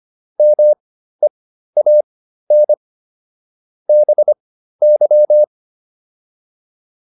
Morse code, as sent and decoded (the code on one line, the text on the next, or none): -- . .- -. / -... -.--
MEAN BY